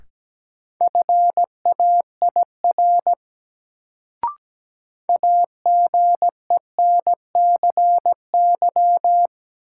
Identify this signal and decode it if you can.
Morse code, heard as a keyed tone, about 17 wpm, 700 Hz. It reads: FAIR AGENCY
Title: Sentences from Top 1300 Words - New Word Drill Alt - 17wpm